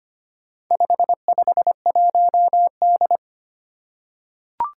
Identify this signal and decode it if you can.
Morse code, heard as a keyed tone, about 25 wpm, 700 Hz. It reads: 551D
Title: Four Challenging Letter-Number Combinations 25wpm